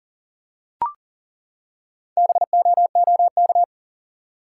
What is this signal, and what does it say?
Morse code, heard as a keyed tone, about 40 wpm, 700 Hz. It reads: BOOK
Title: Four Letter Words - 40wpm